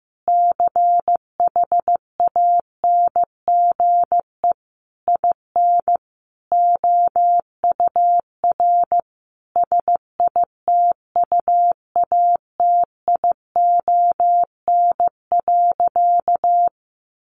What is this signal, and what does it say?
Morse code, heard as a keyed tone, about 15 wpm, 700 Hz. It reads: CHANGE IN OUR SITUATION.
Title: Sentences from Top 1200 Words 15wpm